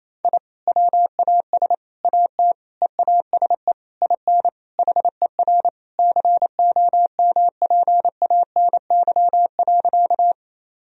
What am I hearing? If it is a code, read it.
Morse code, heard as a keyed tone, about 28 wpm, 700 Hz. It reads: I WAS AT EASE IN HER COMPANY.
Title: Sentences from Top 1000 Words - Review Part 1 - 28wpm